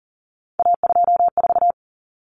Morse code, heard as a keyed tone, about 40 wpm, 700 Hz. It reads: A24
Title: Common Cut Numbers - 3-Digit 40wpm